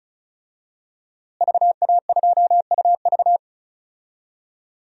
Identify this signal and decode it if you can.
Morse code, heard as a keyed tone, about 36 wpm, 700 Hz. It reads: VA2UV
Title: DXCC-100 Call Signs - 5 Characters - Repeated 2x - 35wpm